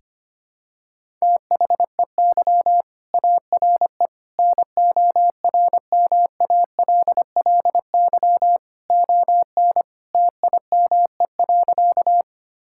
Morse code, heard as a keyed tone, about 25 wpm, 700 Hz. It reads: THEY ARE NORMALLY ON TIME.
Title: Sentences from Top 1000 Words - Review Part 6 - 25wpm